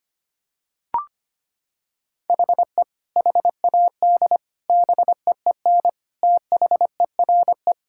Morse code, heard as a keyed tone, about 25 wpm, 700 Hz. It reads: HE HAD BEEN THERE
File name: practice / 25wpm / Sentences with 3-Word Phrases - 25wpm